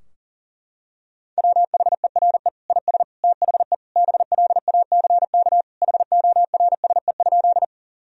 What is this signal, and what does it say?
Morse code, heard as a keyed tone, about 40 wpm, 700 Hz. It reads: WHERE IS THE BLACK HORSE?
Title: Sentences from Top 1000 Words - Review Part 2 - 40wpm